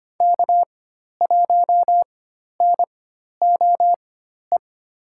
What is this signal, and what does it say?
Morse code, heard as a keyed tone, about 25 wpm, spaced as for 12 wpm, 690 Hz. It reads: K1NOE
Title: USA Call Signs - 4x Character Spacing - 25wpm